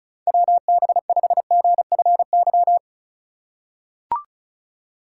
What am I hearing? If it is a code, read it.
Morse code, heard as a keyed tone, about 36 wpm, 700 Hz. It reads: WB5GFY
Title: USA Call Signs - 6 Characters - 2x Character Spacing - 35wpm